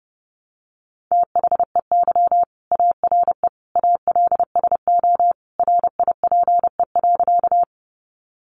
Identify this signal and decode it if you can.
Morse code, heard as a keyed tone, about 30 wpm, 700 Hz. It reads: THEY ARE ALSO RIPE.
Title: Additional Easy Phrases 30wpm